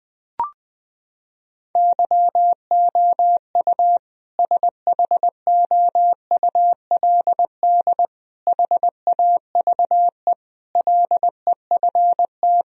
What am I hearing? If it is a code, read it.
Morse code, heard as a keyed tone, about 20 wpm, 700 Hz. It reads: YOU SHOULD HAVE LEFT
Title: Sentences from Top 1000 Words - Review Part 1 - 20wpm